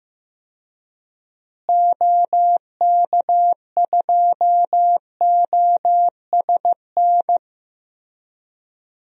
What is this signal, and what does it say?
Morse code, heard as a keyed tone, about 15 wpm, 700 Hz. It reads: OK2OSN
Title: DXCC-100 Call Signs - 6 Characters - Repeated 3x - 15wpm